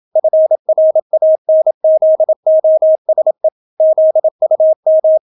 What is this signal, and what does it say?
Morse code, heard as a keyed tone, about 27 wpm, 630 Hz. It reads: FRANZOSE ZUM